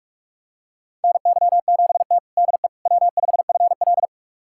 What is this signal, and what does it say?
Morse code, heard as a keyed tone, about 46 wpm, 700 Hz. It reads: NY7T DE W5FL
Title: USA Callsign DE Callsign - 4 Characters - 45wpm